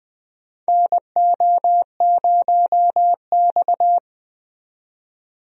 Morse code, heard as a keyed tone, about 20 wpm, 700 Hz. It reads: NO0X
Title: USA Call Signs - 4 Characters - 20wpm